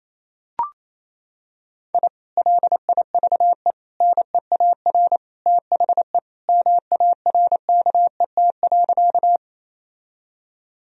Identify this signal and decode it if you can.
Morse code, heard as a keyed tone, about 28 wpm, 700 Hz. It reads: I LIVE NEAR THE MARKET.
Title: Sentences from Top 1000 Words - Review Part 3 - 28wpm